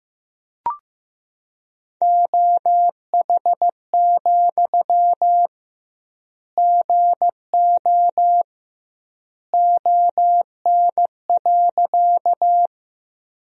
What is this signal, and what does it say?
Morse code, heard as a keyed tone, about 15 wpm, 700 Hz. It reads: OH, GO ON.
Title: 2x Word-Spacing - Sentences from Top 500 Words - 15wpm